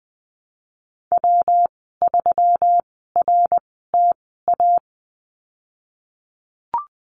Morse code, heard as a keyed tone, about 20 wpm, spaced as for 15 wpm, 700 Hz. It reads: W3RTA
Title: 238 Lesson 38g - Letter q - Call Signs 20wpm